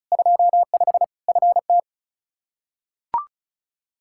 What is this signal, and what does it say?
Morse code, heard as a keyed tone, about 36 wpm, 700 Hz. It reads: 25 FT
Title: Practice QSO - Exchange 2 - Info - 35wpm